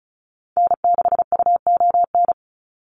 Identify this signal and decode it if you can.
Morse code, heard as a keyed tone, about 34 wpm, 700 Hz. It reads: N6UON
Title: USA Call Signs 35wpm